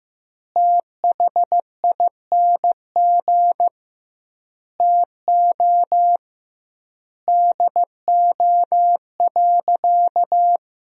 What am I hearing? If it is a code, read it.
Morse code, heard as a keyed tone, about 15 wpm, 700 Hz. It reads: THING TO DO.
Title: 2x Word-Spacing - Sentences from Top 100 Words - 15wpm